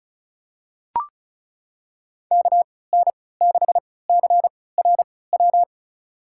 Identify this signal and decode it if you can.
Morse code, heard as a keyed tone, about 36 wpm, spaced as for 20 wpm, 700 Hz. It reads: KN6CRW
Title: USA Call Signs - 6 Characters - 3x Character Spacing - 35wpm